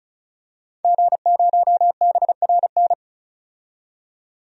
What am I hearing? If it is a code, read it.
Morse code, heard as a keyed tone, about 36 wpm, 700 Hz. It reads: G0BRN
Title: DXCC-100 Call Signs - 5 Characters - 35wpm